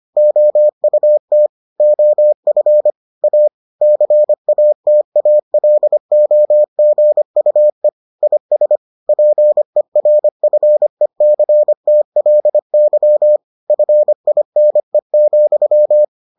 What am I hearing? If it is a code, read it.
Morse code, heard as a keyed tone, about 25 wpm, 600 Hz. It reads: OUT OF A CATALOGUE IS PERFECTLY FINE,